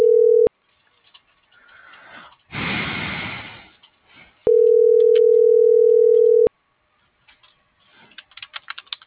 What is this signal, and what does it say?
Ringback tone (North America), 440 + 480 Hz.